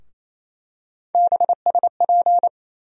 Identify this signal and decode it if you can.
Morse code, heard as a keyed tone, about 28 wpm, 700 Hz. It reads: BSP
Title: USA Call Sign Postfixes - 3 Letters 28wpm